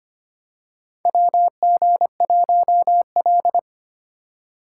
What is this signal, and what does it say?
Morse code, heard as a keyed tone, about 25 wpm, 700 Hz. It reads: WG1L
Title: USA Call Signs - 4 Characters - 3x Character Spacing - 25wpm